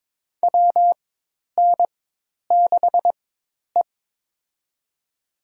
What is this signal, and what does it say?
Morse code, heard as a keyed tone, about 22 wpm, spaced as for 10 wpm, 700 Hz. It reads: WN6E
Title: USA Call Signs - 4 Characters - 4x Character Spacing - 22wpm